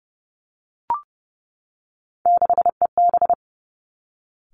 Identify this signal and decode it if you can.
Morse code, heard as a keyed tone, about 30 wpm, 700 Hz. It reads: 6EB
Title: Three Challenging Letter-Number Combinations 30wpm